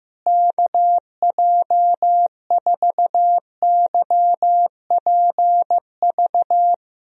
Morse code, heard as a keyed tone, about 15 wpm, 700 Hz. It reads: KJ4YPV
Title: USA Call Signs - 6 Characters - Repeated 3x - 15wpm